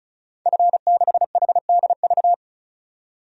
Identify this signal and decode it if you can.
Morse code, heard as a keyed tone, about 36 wpm, 700 Hz. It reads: F6HDV
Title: DXCC-100 Call Signs - 5 Characters - 35wpm